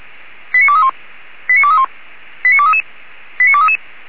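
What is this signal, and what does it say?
Five-tone selective call (ZVEI), heard as quick runs of five tones, tones of about 80 ms. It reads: 89231 89231 89233 89233
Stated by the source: ZVEI five-tone selective call received by SDR.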